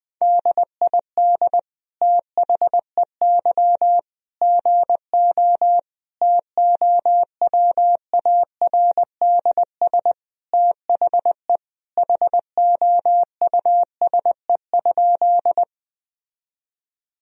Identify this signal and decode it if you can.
Morse code, heard as a keyed tone, about 19 wpm, 700 Hz. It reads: DID THEY GO TOWARDS THE HOUSE?
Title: Sentences from Top 1000 Words - Review Part 2 - 20wpm